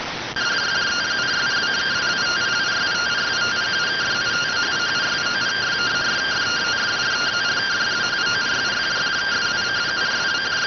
Two-tone teleprinter signal, about 170 Hz shift.